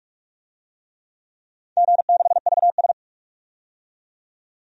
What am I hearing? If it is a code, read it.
Morse code, heard as a keyed tone, about 44 wpm, 690 Hz. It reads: G6VS
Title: DXCC-100 Call Signs - 4 Characters - 45wpm